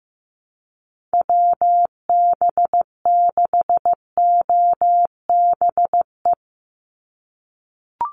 Morse code, heard as a keyed tone, about 15 wpm, 700 Hz. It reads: WB6OBE
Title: USA Call Signs 15wpm